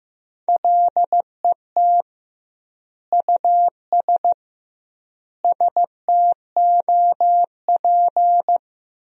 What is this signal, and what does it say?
Morse code, heard as a keyed tone, about 15 wpm, 700 Hz. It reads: LET US STOP
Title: 2x Word-Spacing - Sentences from Top 400 Words - 15wpm